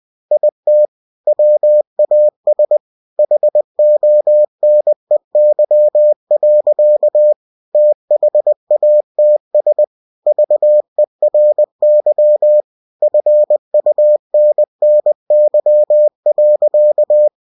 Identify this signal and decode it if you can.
Morse code, heard as a keyed tone, about 20 wpm, 600 Hz. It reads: IT WAS HONEY. THATS VERY FUNNY.
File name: cw-books / winnie-the-pooh / winnie-20wpm-page-0039